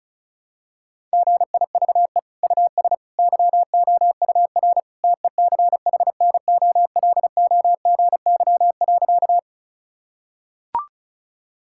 Morse code, heard as a keyed tone, about 36 wpm, 700 Hz. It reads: GIVE US YOUR TECHNOLOGY.
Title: Sentences from Top 1000 Words - Review Part 5 - 35wpm